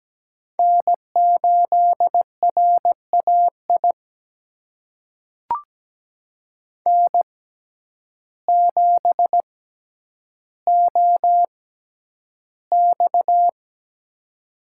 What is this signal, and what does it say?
Morse code, heard as a keyed tone, about 17 wpm, spaced as for 9 wpm, 700 Hz. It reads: N8RAI N7OX
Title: USA Call Signs - 6x Character Spacing - 17wpm